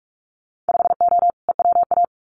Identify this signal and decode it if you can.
Morse code, heard as a keyed tone, about 44 wpm, 700 Hz. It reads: 5O EWA
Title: Winter Field Day Exchanges Encore 45wpm